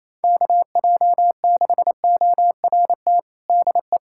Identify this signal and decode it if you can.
Morse code, heard as a keyed tone, about 27 wpm, 700 Hz. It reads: KJ6ORT DE
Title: USA Callsign DE Callsign - 6 Characters - 28wpm